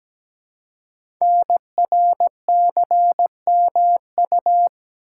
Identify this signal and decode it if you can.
Morse code, heard as a keyed tone, about 17 wpm, 700 Hz. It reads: NRCMU
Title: Code Groups - 5 Letters - 17wpm